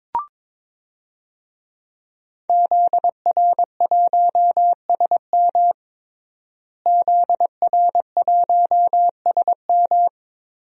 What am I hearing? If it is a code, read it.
Morse code, heard as a keyed tone, about 22 wpm, 700 Hz. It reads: ZR1SM ZR1SM
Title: DXCC-100 Call Signs - 5 Characters - Repeated 3x - 22wpm